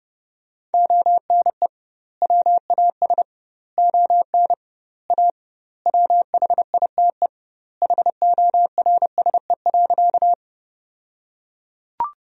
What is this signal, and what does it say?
Morse code, heard as a keyed tone, about 30 wpm, 700 Hz. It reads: ONE WAS ON A WHITE HORSE.
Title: 2x Word-Spacing - Sentences from Top 300 Words - 30wpm